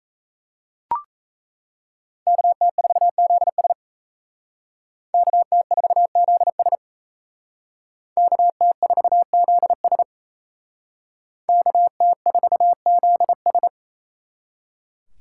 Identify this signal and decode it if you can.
Morse code, heard as a keyed tone, about 34 wpm, 700 Hz. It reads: KT4ZS KT4ZS KT4ZS KT4ZS
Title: Speed Racing - USA 2x2 Call Signs 28wpm